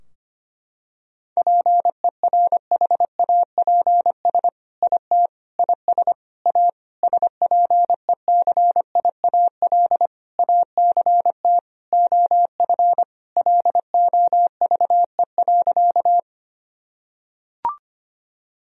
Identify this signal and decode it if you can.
Morse code, heard as a keyed tone, about 25 wpm, 700 Hz. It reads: PERHAPS IT IS A SPECIAL ACT OF LOVE.
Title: Sentences from Top 1000 Words - Review Part 3 - 25wpm